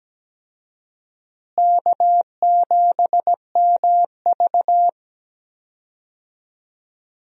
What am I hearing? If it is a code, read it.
Morse code, heard as a keyed tone, about 17 wpm, 700 Hz. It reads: K7MV
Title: USA Call Signs - 4 Characters - Repeated 2x - 17wpm